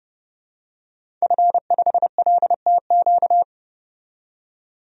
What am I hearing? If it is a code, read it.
Morse code, heard as a keyed tone, about 30 wpm, 700 Hz. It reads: F5LTQ
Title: DXCC-100 Call Signs - 5 Characters - 30wpm